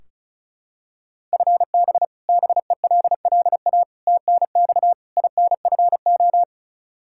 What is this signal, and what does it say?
Morse code, heard as a keyed tone, about 36 wpm, 700 Hz. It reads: FB BELLA TNX INFO
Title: Practice QSO - Exchange 3 - More Info - 35wpm